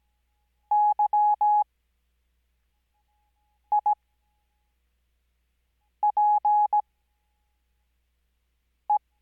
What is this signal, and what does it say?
Morse code, heard as a keyed tone, about 17 wpm, spaced as for 4 wpm, 840 Hz. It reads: YIPE
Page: Farnsworth Spacing: 15WPM Character speed Spaced at 5WPM.